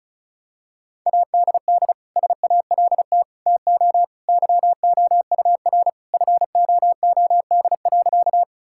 Morse code, heard as a keyed tone, about 34 wpm, 700 Hz. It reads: ADD SALT TO YOUR FOOD.
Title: Sentences from Top 1000 Words - Review Part 5 - 35wpm